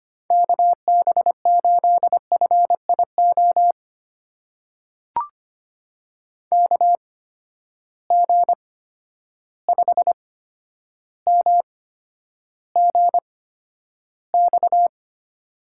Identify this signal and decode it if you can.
Morse code, heard as a keyed tone, about 25 wpm, spaced as for 11 wpm, 700 Hz. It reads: KB8FIO KG5MGX
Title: USA Call Signs - 6 Characters - 8x Character Spacing - 25wpm